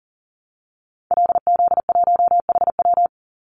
Morse code, heard as a keyed tone, about 37 wpm, 700 Hz. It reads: LZ1HW
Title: Call Signs Repeated 2x - 40wpm